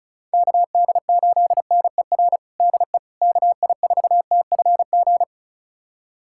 Morse code, heard as a keyed tone, about 36 wpm, 700 Hz. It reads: KD8NER DE KI4TFG
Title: USA Callsign DE Callsign - 6 Characters - 35wpm